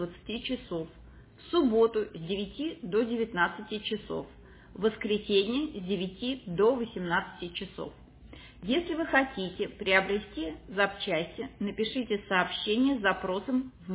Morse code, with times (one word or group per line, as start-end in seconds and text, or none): none